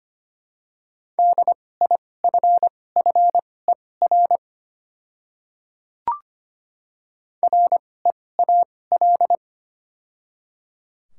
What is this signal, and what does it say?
Morse code, heard as a keyed tone, about 25 wpm, spaced as for 18 wpm, 700 Hz. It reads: DIFFER REAL
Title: 2x Character Spacing - Top 300 Words - 25wpm